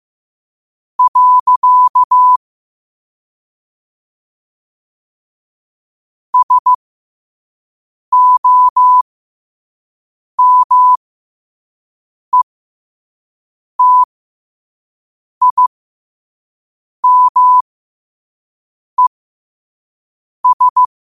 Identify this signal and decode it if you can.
Morse code, heard as a keyed tone, about 15 wpm, spaced as for 5 wpm, 1000 Hz. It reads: . SOMETIMES
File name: qotdfiles / 5wpm